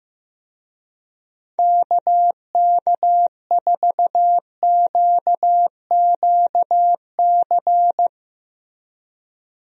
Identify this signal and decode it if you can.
Morse code, heard as a keyed tone, about 15 wpm, 700 Hz. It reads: KK4QQC
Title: USA Call Signs - 6 Characters - Repeated 3x - 15wpm